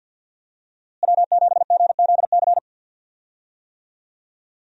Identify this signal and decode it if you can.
Morse code, heard as a keyed tone, about 46 wpm, 700 Hz. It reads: W7GZC
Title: USA Call Signs - 5 Characters - Repeated 2x - 50wpm